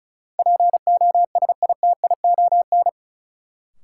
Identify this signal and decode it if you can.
Morse code, heard as a keyed tone, about 36 wpm, 700 Hz. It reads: POSITION
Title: Sentences from Top 900 Words - New Word Drill - 35wpm